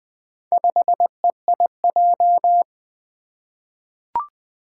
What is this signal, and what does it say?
Morse code, heard as a keyed tone, about 20 wpm, 700 Hz. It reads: 5EIJ
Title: Four Challenging Letter-Number Combinations 20wpm